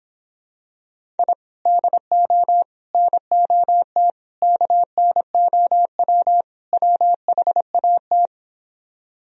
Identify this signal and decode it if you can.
Morse code, heard as a keyed tone, about 26 wpm, 700 Hz. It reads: I DO NOT KNOW WHAT